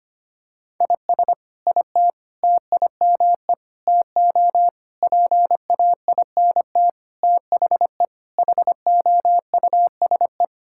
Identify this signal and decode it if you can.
Morse code, heard as a keyed tone, about 25 wpm, 700 Hz. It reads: IS IT TIME TO PAINT THE HOUSE
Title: Sentences from Top 1000 Words - Review Part 6 - 25wpm